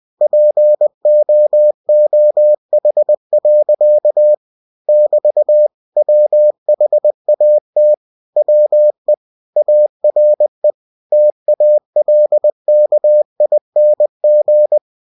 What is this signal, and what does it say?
Morse code, heard as a keyed tone, about 20 wpm, 600 Hz. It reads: POOH. <BT> WHAT WE ARE TALKING
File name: cw-books / winnie-the-pooh / winnie-20wpm-page-0089